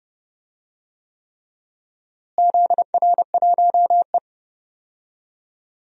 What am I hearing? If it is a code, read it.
Morse code, heard as a keyed tone, about 31 wpm, 700 Hz. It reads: ZR1E
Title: DXCC-100 Call Signs - 4 Characters - Repeated 3x - 30wpm